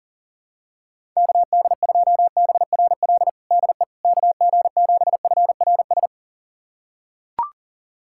Morse code, heard as a keyed tone, about 37 wpm, 700 Hz. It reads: KD2BRL DE KG7FRS
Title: USA Callsign DE Callsign 40wpm